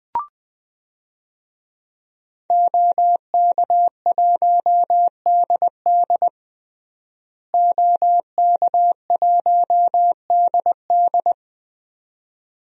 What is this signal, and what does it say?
Morse code, heard as a keyed tone, about 20 wpm, 700 Hz. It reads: OK1DD OK1DD
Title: DXCC-100 Call Signs - 5 Characters - Repeated 2x - 20wpm